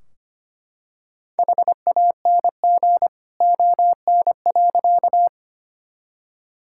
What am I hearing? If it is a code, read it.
Morse code, heard as a keyed tone, about 25 wpm, 700 Hz. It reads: HANG ON.
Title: Sentences from Top 1600 Words 25wpm